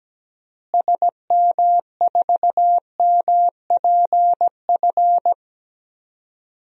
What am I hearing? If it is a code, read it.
Morse code, heard as a keyed tone, about 17 wpm, 700 Hz. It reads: SM4MPF
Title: DXCC-100 Call Signs - 6 Characters - Repeated 2x - 17wpm